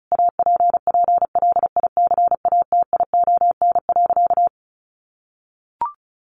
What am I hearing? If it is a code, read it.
Morse code, heard as a keyed tone, about 34 wpm, 700 Hz. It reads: APPLICATION.
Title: Sentences from Top 1200 Words 35wpm